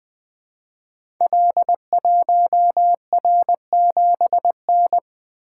Morse code, heard as a keyed tone, about 20 wpm, 700 Hz. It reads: L1R7N
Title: Code Groups - 5 Letter:Numbers - 20wpm